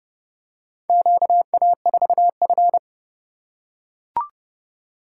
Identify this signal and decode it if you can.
Morse code, heard as a keyed tone, about 28 wpm, 700 Hz. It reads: QA4F